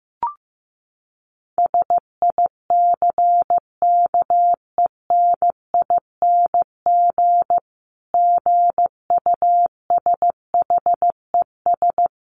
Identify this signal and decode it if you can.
Morse code, heard as a keyed tone, about 15 wpm, 700 Hz. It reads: SICKENING GUSHES
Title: A Sound Of Thunder 15wpm